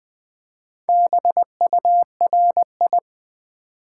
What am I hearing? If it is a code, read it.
Morse code, heard as a keyed tone, about 20 wpm, 700 Hz. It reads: BURI
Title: Build A Word - 2 Letters At A Time - 6-Letter Words - 20wpm